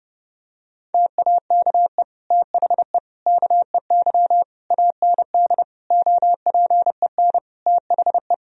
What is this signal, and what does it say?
Morse code, heard as a keyed tone, about 32 wpm, 700 Hz. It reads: TAKE THE KEY AND OPEN THE
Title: Sentences from Top 1000 Words - Review Part 4 - 30wpm